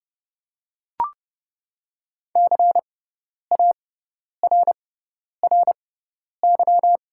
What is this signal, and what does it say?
Morse code, heard as a keyed tone, about 29 wpm, spaced as for 10 wpm, 700 Hz. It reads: CARRY
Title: 6x Character Spacing - Top 300 Words - 30wpm